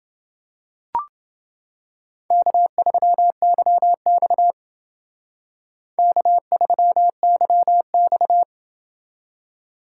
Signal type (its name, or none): Morse code